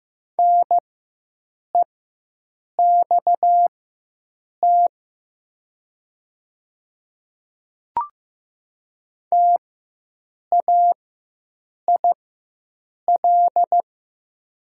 Morse code, heard as a keyed tone, about 15 wpm, spaced as for 7 wpm, 700 Hz. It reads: NEXT TAIL
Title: 4x Character Spacing - Top 500 Words - 15wpm